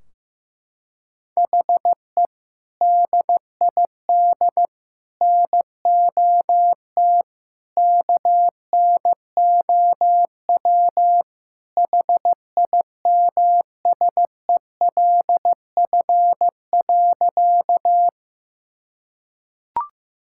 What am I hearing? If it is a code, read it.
Morse code, heard as a keyed tone, about 15 wpm, 700 Hz. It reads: HE DID NOT KNOW HIMSELF.
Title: Sentences from Top 1600 Words 15wpm